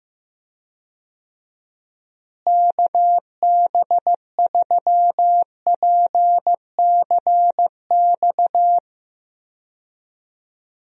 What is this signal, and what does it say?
Morse code, heard as a keyed tone, about 15 wpm, 700 Hz. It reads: KB3PCX
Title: USA Call Signs - 6 Characters - Repeated 3x - 15wpm